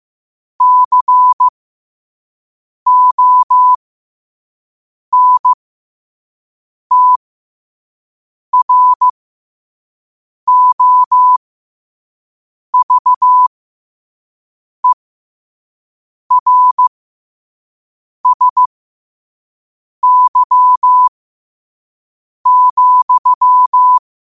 Morse code, heard as a keyed tone, about 15 wpm, spaced as for 5 wpm, 1000 Hz. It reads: CONTROVERSY,
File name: qotdfiles / 5wpm